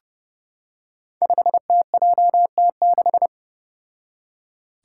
Morse code, heard as a keyed tone, about 30 wpm, 700 Hz. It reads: 5TJT6
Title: Five Challenging Letter-Number Combinations 30wpm